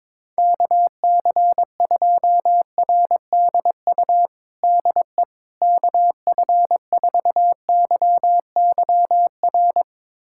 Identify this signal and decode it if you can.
Morse code, heard as a keyed tone, about 22 wpm, 700 Hz. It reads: KC2RDU DE KF4YYR
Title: USA Callsign DE Callsign 22wpm